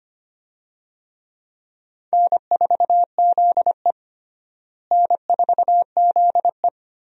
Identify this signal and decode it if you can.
Morse code, heard as a keyed tone, about 25 wpm, 700 Hz. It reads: N4ZE N4ZE
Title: USA Call Signs - 4 Characters - Repeated 3x - 25wpm